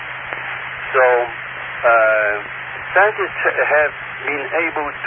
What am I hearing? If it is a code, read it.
Emergency Alert System (SAME) tones, heard as harsh buzzing bursts, with speech. It is heard from 0.4 to 4.3 s.